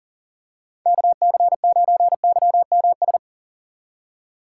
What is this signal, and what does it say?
Morse code, heard as a keyed tone, about 37 wpm, 700 Hz. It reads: KC9YMS
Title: USA Call Signs - 6 Characters - 2x Character Spacing - 40wpm